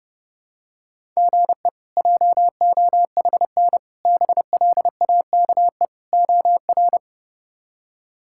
Morse code, heard as a keyed tone, about 32 wpm, 700 Hz. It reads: GE JOHN BLAKE OR
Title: K1USN SST QSO Practice - 30wpm